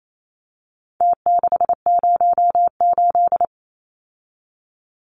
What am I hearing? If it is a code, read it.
Morse code, heard as a keyed tone, about 28 wpm, 700 Hz. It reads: T608
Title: Numbers - Contest Style - 4-Digit 28wpm